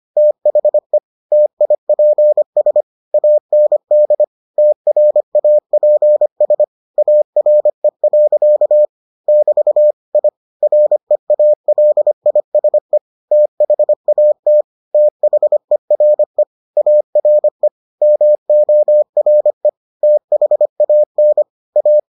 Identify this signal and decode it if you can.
Morse code, heard as a keyed tone, about 25 wpm, 600 Hz. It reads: THE TIPS AND TRAPS ARE. <BT> I REALISE THAT THERE ARE MORE THAN A